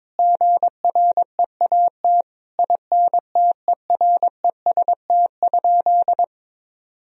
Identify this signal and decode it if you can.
Morse code, heard as a keyed tone, about 22 wpm, 700 Hz. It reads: GREAT INTEREST?
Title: Sentences from Top 1000 Words - Review Part 2 - 22wpm